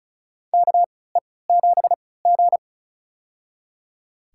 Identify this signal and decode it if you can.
Morse code, heard as a keyed tone, about 36 wpm, spaced as for 20 wpm, 700 Hz. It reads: KE7G
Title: USA Call Signs - 3x Character Spacing - 35wpm